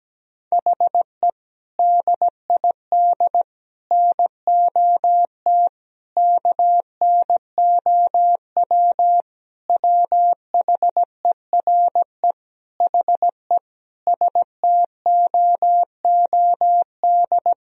Morse code, heard as a keyed tone, about 17 wpm, 700 Hz. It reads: HE DID NOT KNOW WHERE HE STOOD
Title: Sentences from Top 1000 Words - Review Part 2 - 17wpm